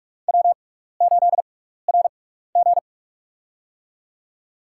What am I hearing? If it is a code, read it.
Morse code, heard as a keyed tone, about 46 wpm, spaced as for 15 wpm, 700 Hz. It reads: W8RG
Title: USA Call Signs - 4 Characters - 6x Character Spacing - 45wpm